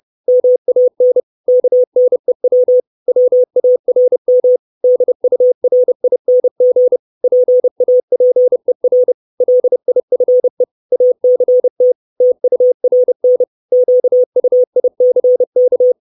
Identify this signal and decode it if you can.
Morse code, heard as a keyed tone, about 30 wpm, 500 Hz. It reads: MAN KNEW WARM DURING PAPER LIFE ACT TURN QUICK
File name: 632 Words 30WPM